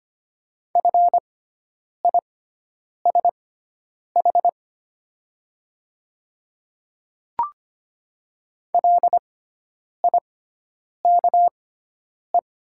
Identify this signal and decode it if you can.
Morse code, heard as a keyed tone, about 25 wpm, spaced as for 9 wpm, 700 Hz. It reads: FISH LIKE
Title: 6x Character Spacing - Top 300 Words - 25wpm